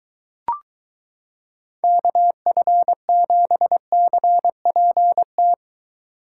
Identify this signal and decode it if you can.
Morse code, heard as a keyed tone, about 22 wpm, 700 Hz. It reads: KF7CPT